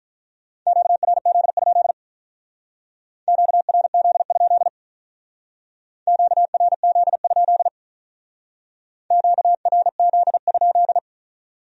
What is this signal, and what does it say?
Morse code, heard as a keyed tone, about 43 wpm, 700 Hz. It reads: QRZ? QRZ? QRZ? QRZ?
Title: Speed-Racing Q-Codes - 35wpm